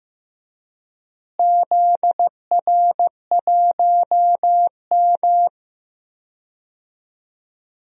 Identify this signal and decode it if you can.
Morse code, heard as a keyed tone, about 15 wpm, 700 Hz. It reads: ZR1M
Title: DXCC-100 Call Signs - 4 Characters - Repeated 3x - 15wpm